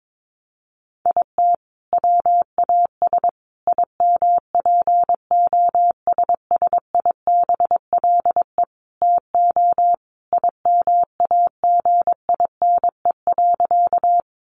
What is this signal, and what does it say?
Morse code, heard as a keyed tone, about 22 wpm, 700 Hz. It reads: IT WAS IMPOSSIBLE TO IMAGINE.
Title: Sentences from Top 1400 Words 22wpm